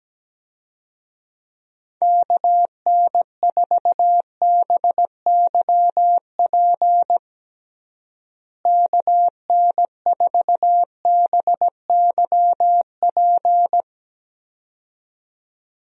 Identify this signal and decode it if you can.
Morse code, heard as a keyed tone, about 17 wpm, 700 Hz. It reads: KN4BYP KN4BYP
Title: USA Call Signs - 6 Characters - Repeated 2x - 17wpm